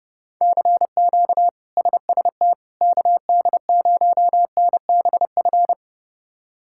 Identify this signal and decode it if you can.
Morse code, heard as a keyed tone, about 30 wpm, 700 Hz. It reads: CQ SST KD0NBF
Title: K1USN SST QSO Practice - 30wpm